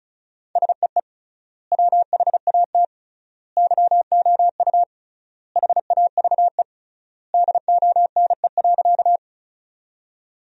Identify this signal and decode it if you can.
Morse code, heard as a keyed tone, about 34 wpm, 700 Hz. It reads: SEE WHAT YOU HAVE DONE.
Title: 3x Word-Spacing - Sentences from Top 500 Words - 35wpm